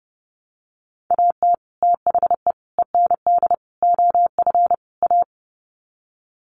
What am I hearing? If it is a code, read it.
Morse code, heard as a keyed tone, about 30 wpm, 700 Hz. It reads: AT THE END OF A
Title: Sets of 5 Words 30wpm